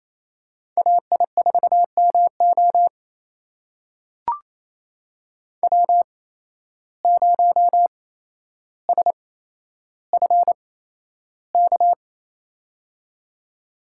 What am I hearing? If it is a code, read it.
Morse code, heard as a keyed tone, about 28 wpm, spaced as for 12 wpm, 700 Hz. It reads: AI4MO W0SFK
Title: USA Call Signs - 5 Characters - 8x Character Spacing - 28wpm